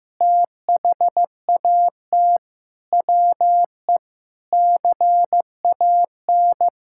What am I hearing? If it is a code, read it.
Morse code, heard as a keyed tone, about 16 wpm, 700 Hz. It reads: THAT WE CAN
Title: Sentences from Top 1000 Words - Review Part 6 - 15wpm